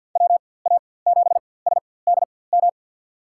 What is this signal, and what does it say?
Morse code, heard as a keyed tone, about 51 wpm, spaced as for 23 wpm, 700 Hz. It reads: WA7SDM
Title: USA Call Signs - 4x Character Spacing - 50wpm